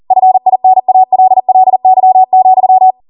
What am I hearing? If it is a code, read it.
Morse code, heard as a keyed tone, about 40 wpm, 750 Hz. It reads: FINALLY,